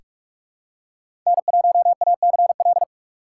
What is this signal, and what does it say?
Morse code, heard as a keyed tone, about 41 wpm, 700 Hz. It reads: N1ACL
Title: USA Call Signs - 5 Characters - 2x Character Spacing - 45wpm